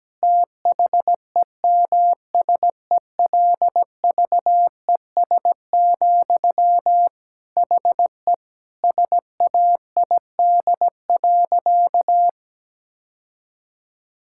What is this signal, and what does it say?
Morse code, heard as a keyed tone, about 17 wpm, 700 Hz. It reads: THEMSELVES, HE SAID.
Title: Sentences from Top 1000 Words - Review Part 6 - 17wpm